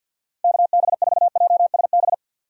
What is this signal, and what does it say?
Morse code, heard as a keyed tone, about 47 wpm, 700 Hz. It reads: KB4JSB
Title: USA Call Signs - 6 Characters - 50wpm